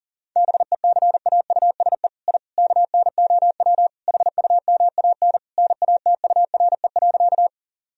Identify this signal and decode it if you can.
Morse code, heard as a keyed tone, about 40 wpm, 700 Hz. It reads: BECAUSE I KNOW HUMAN NATURE.
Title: Sentences from Top 1000 Words - Review Part 3 - 40wpm